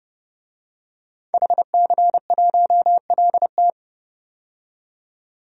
Morse code, heard as a keyed tone, about 30 wpm, 700 Hz. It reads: HC1LT
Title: DXCC-100 Call Signs - 5 Characters - 30wpm